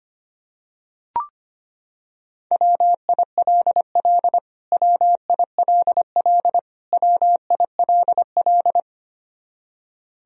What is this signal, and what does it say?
Morse code, heard as a keyed tone, about 25 wpm, 700 Hz. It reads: WILL WILL WILL
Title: Top 100 Words Repeated 25wpm